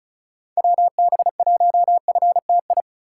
Morse code, heard as a keyed tone, about 36 wpm, 700 Hz. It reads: WB1FTI
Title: USA Call Signs - 6 Characters - 4x Character Spacing - 35wpm